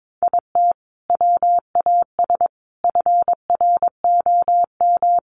Morse code, heard as a keyed tone, about 22 wpm, 700 Hz. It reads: IT WAS FROM
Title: Sentences from Top 800 Words 22wpm